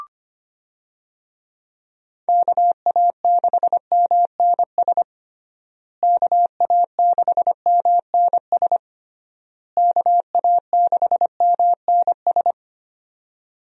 Morse code, heard as a keyed tone, about 25 wpm, 700 Hz. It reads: KA6MNS KA6MNS KA6MNS
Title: USA Call Signs - 6 Characters - Repeated 3x - 25wpm